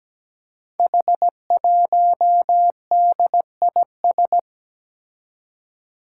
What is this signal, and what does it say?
Morse code, heard as a keyed tone, about 17 wpm, 700 Hz. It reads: H1DIS
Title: Five Challenging Letter-Number Combinations 17wpm